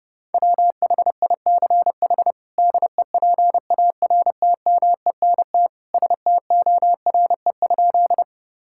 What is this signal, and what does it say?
Morse code, heard as a keyed tone, about 30 wpm, 700 Hz. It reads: WHICH DEPARTMENT STORE?
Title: Sentences from Top 1000 Words - Review Part 5 - 30wpm